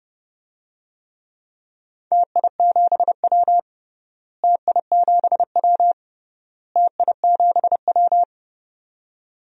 Morse code, heard as a keyed tone, about 31 wpm, 700 Hz. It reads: TI7W TI7W TI7W
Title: DXCC-100 Call Signs - 4 Characters - Repeated 3x - 30wpm